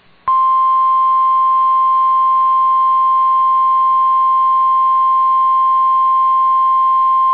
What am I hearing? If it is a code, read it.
Emergency Alert System (SAME) tones (the 1050 Hz attention tone). It is heard from 0.4 to 6.4 s.